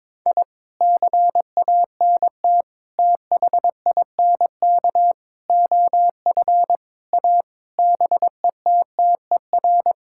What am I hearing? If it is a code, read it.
Morse code, heard as a keyed tone, about 22 wpm, 700 Hz. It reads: I CANT THINK OF A BETTER
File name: practice / 22wpm / Sets of 8 Words - 22wpm